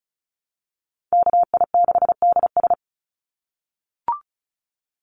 Morse code, heard as a keyed tone, about 34 wpm, 700 Hz. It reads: KI6DS
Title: NA SOTA Chaser Callsigns 35wpm